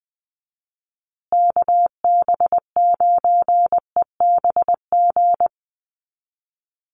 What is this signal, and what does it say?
Morse code, heard as a keyed tone, about 20 wpm, 700 Hz. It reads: KB9EBG
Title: USA Call Signs Repeated 2x 20wpm